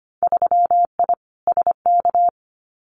Morse code, heard as a keyed tone, about 25 wpm, 700 Hz. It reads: 3I SK
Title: Winter Field Day Exchanges Encore 25wpm